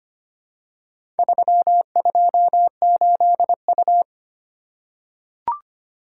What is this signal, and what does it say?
Morse code, heard as a keyed tone, about 25 wpm, 700 Hz. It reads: 328U
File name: practice / 25wpm / Four Letter-Number Combinations - 25wpm